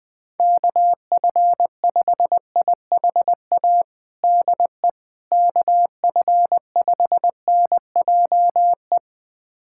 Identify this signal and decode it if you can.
Morse code, heard as a keyed tone, about 21 wpm, 700 Hz. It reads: KF5IHA DE KF5NJE
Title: USA Callsign DE Callsign 20wpm